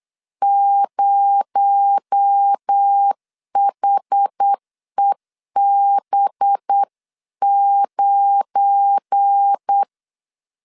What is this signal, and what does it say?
Morse code, heard as a keyed tone, about 8 wpm, 800 Hz. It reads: 0HEB9
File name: LEZIONI / 7 GF9Z / 7 GF9 8wpm 3545